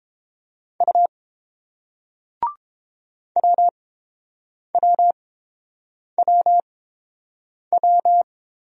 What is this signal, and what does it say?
Morse code, heard as a keyed tone, about 31 wpm, spaced as for 7 wpm, 700 Hz. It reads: U WWWW